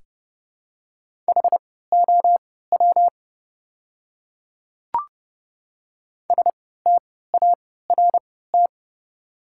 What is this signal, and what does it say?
Morse code, heard as a keyed tone, about 30 wpm, spaced as for 17 wpm, 700 Hz. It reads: HOW START
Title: 3x Character Spacing - Top 300 Words - 30wpm